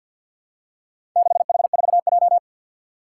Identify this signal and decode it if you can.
Morse code, heard as a keyed tone, about 51 wpm, 700 Hz. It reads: 6H4J
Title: Four Challenging Letter-Number Combinations 50wpm